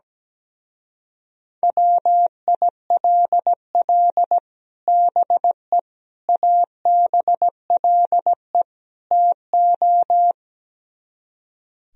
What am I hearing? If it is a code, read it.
Morse code, heard as a keyed tone, about 17 wpm, 700 Hz. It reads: WILL BE ABLE TO
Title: Speed-Racing - Sets of 4 Words 17wpm